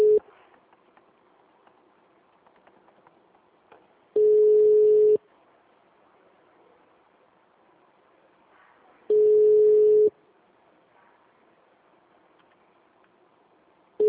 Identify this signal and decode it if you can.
Ringback tone (Europe, 425 Hz).
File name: out- -2015- - - .3262